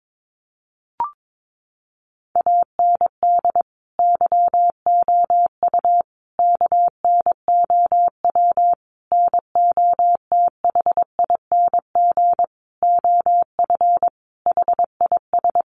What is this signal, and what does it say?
Morse code, heard as a keyed tone, about 22 wpm, 700 Hz. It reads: AND YOU KNOW NOTHING OF HIS
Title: Sentences from Top 1200 Words 22wpm